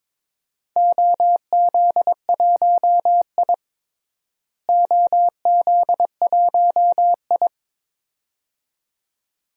Morse code, heard as a keyed tone, about 22 wpm, 700 Hz. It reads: OZ1I OZ1I
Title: DXCC-100 Call Signs - 4 Characters - Repeated 2x - 22wpm